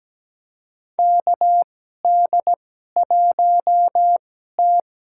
Morse code, heard as a keyed tone, about 17 wpm, spaced as for 12 wpm, 700 Hz. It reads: KD1T
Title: USA Call Signs - 4 Characters - 2x Character Spacing - 17wpm